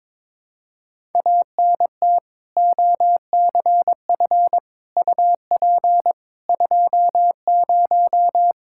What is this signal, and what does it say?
Morse code, heard as a keyed tone, about 22 wpm, 700 Hz. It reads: ANT OCF UP 20
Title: Practice QSO - Exchange 2 - Info - 22wpm